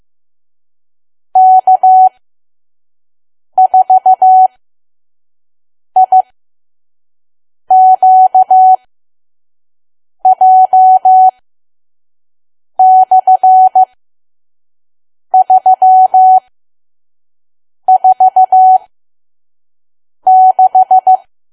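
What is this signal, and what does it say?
Morse code, heard as a keyed tone, about 15 wpm, spaced as for 5 wpm, 750 Hz. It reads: K4IQJ/346
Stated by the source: Morse code practice at 5 WPM from KQ2P.COM.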